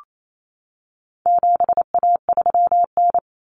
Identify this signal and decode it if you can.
Morse code, heard as a keyed tone, about 28 wpm, 700 Hz. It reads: 7A3N
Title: Common Cut Numbers - 4-Digit 28wpm